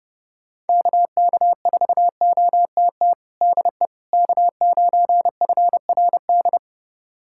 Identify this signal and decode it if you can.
Morse code, heard as a keyed tone, about 32 wpm, 700 Hz. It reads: KK4OTT DE K9FRD
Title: USA Callsign DE Callsign 30wpm